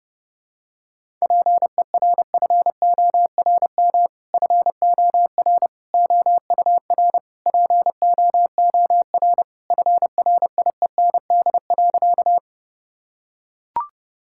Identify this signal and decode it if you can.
Morse code, heard as a keyed tone, about 32 wpm, 700 Hz. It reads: PERFORM FOR OUR POOR FRIEND.
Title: Sentences from Top 1000 Words - Review Part 5 - 30wpm